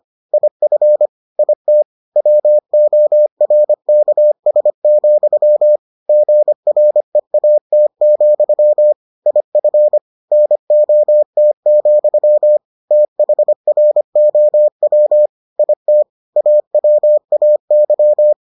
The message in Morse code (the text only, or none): IF IT WORKS, GREAT, IF NOT, THROW IT AWAY